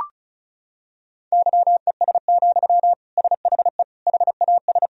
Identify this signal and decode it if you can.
Morse code, heard as a keyed tone, about 34 wpm, 700 Hz. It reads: YES, SHE HAS
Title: Sentences from Top 1000 Words - Review Part 2 - 35wpm